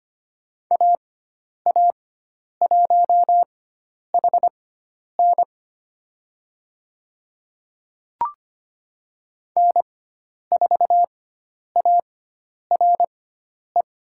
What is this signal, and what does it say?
Morse code, heard as a keyed tone, about 25 wpm, spaced as for 10 wpm, 700 Hz. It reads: AA1HN N4ARE
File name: Lesson 16g - Letter c - Call Signs 25wpm